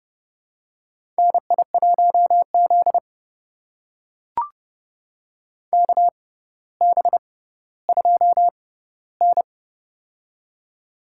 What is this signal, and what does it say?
Morse code, heard as a keyed tone, about 30 wpm, spaced as for 15 wpm, 700 Hz. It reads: NI1Z KB2N
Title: USA Call Signs - 4 Characters - 6x Character Spacing - 30wpm